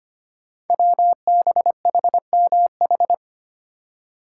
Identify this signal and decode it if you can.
Morse code, heard as a keyed tone, about 25 wpm, 700 Hz. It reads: WBHMH
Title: Code Groups - 5 Letter:Numbers - 25wpm